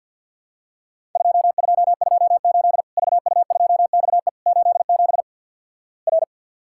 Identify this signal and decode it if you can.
Morse code, heard as a keyed tone, about 50 wpm, 700 Hz. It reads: 2228 VU2XE 87 R
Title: Worked All Europe DX Contest - QTC Exchange - 50wpm